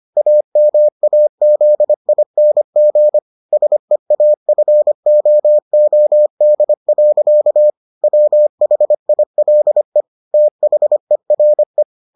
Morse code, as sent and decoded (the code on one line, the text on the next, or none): .- -- .- --.. .. -. --. / ... .- ..-. --- --- -.. .-.-.- / .-- .... .. .-.. / - .... .-. .
AMAZING SEAFOOD. WHILE THERE